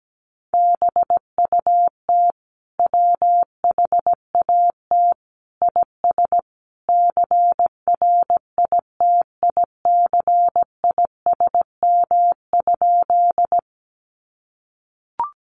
Morse code, heard as a keyed tone, about 17 wpm, 700 Hz. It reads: BUT WHAT IS CRITICISM?
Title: Sentences from Top 1300 Words 17wpm